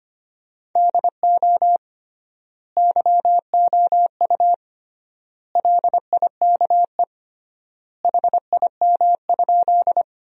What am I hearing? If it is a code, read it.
Morse code, heard as a keyed tone, about 25 wpm, 700 Hz. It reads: DO YOU LIKE HIM?
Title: 3x Word-Spacing - Sentences from Top 100 Words - 25wpm